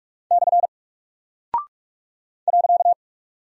Morse code, heard as a keyed tone, about 48 wpm, 700 Hz. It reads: / .
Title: Single Character (Letter:Number:Punctuation:Prosign) Alt 45wpm